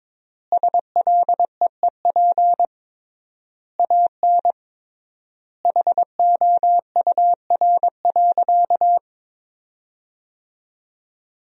Morse code, heard as a keyed tone, about 22 wpm, 700 Hz. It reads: SLEEP AN HOUR.
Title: 3x Word-Spacing - Sentences from Top 500 Words - 22wpm